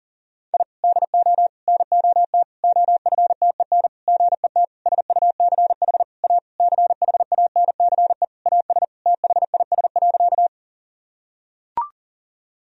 Morse code, heard as a keyed tone, about 37 wpm, 700 Hz. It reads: I DO NOT OFTEN GET SUCH A CHANCE AS THIS.
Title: Sentences from Top 1000 Words - Review Part 1 - 40wpm